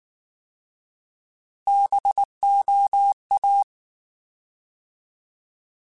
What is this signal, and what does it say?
Morse code, heard as a keyed tone, about 19 wpm, 800 Hz. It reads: BOA
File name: Alphabetical / Alphabetical mp3s / Bs @18